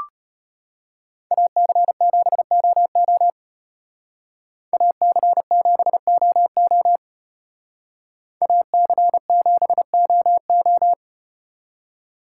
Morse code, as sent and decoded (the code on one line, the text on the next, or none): .- -.-. --... --- --- / .- -.-. --... --- --- / .- -.-. --... --- ---
AC7OO AC7OO AC7OO